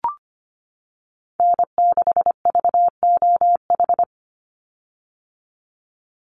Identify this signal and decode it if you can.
Morse code, heard as a keyed tone, about 25 wpm, 700 Hz. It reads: N6VOH
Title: Call Signs - 25wpm